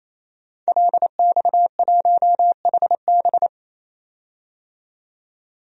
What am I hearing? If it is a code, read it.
Morse code, heard as a keyed tone, about 27 wpm, 700 Hz. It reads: LX1HB
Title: DXCC-100 Call Signs - 5 Characters - 28wpm